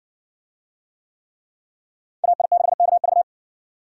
Morse code, heard as a keyed tone, about 60 wpm, 700 Hz. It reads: AI6KV